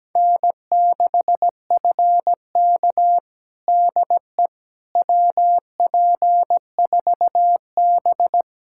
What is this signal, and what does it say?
Morse code, heard as a keyed tone, about 17 wpm, 700 Hz. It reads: N6FK DE WP4B
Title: USA Callsign DE Callsign - 4 Characters - 17wpm